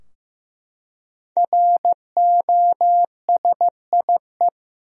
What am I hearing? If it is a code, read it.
Morse code, heard as a keyed tone, about 16 wpm, 700 Hz. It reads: ROSIE
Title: USA Common Names - Max 5 Letters - 15wpm